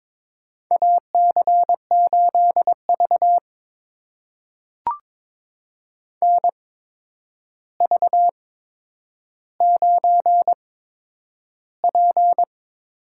Morse code, heard as a keyed tone, about 22 wpm, spaced as for 9 wpm, 700 Hz. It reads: AC8V NV9P
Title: USA Call Signs - 4 Characters - 8x Character Spacing - 22wpm